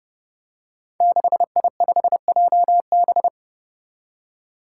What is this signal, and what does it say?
Morse code, heard as a keyed tone, about 29 wpm, 700 Hz. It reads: 6I5JB
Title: Five Challenging Letter-Number Combinations 30wpm